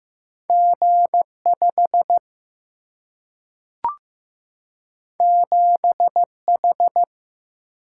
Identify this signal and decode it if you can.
Morse code, heard as a keyed tone, about 15 wpm, 700 Hz. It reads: G 5 7 H
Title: Single-Letter Single-Number 15wpm